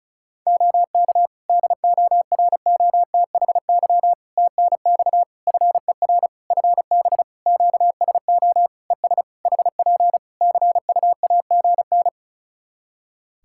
Morse code, heard as a keyed tone, about 36 wpm, 700 Hz. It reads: OK DOROTHY TNX FER FB QSO ES HP CUAGN
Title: Practice QSO - Exchange 4 - Ending - 35wpm